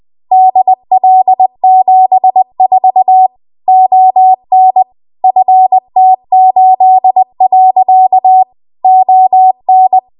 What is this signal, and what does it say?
Morse code, heard as a keyed tone, about 20 wpm, 750 Hz. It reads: DL74 ON FT8. ON
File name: Archive / 20 WPM / 20WPM